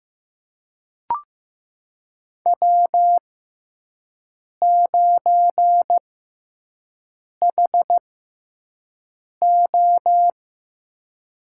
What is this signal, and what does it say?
Morse code, heard as a keyed tone, about 15 wpm, spaced as for 5 wpm, 700 Hz. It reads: W9HO
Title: USA Call Signs - 4 Characters - 6x Character Spacing - 15wpm